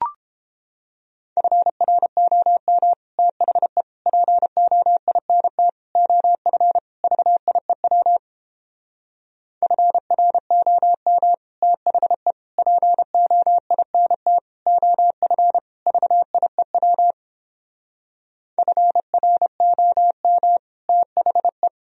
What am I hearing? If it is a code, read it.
Morse code, heard as a keyed tone, about 31 wpm, 700 Hz. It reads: FROM THE POINT OF VIEW FROM THE POINT OF VIEW FROM THE